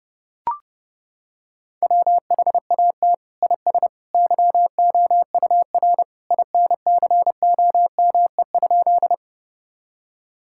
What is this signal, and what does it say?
Morse code, heard as a keyed tone, about 30 wpm, 700 Hz. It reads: WHAT IS YOUR INCOME?
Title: Sentences from Top 1000 Words - Review Part 5 - 30wpm